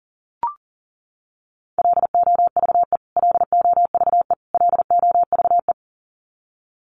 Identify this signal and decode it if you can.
Morse code, heard as a keyed tone, about 40 wpm, 700 Hz. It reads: LOVE LOVE LOVE
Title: Top 500 Words Repeated 40wpm